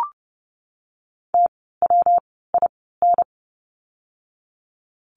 Morse code, heard as a keyed tone, about 30 wpm, 700 Hz. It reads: T W I N
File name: course / 30:10wpm / 115 Lesson 19f - Letter w - Words 30wpm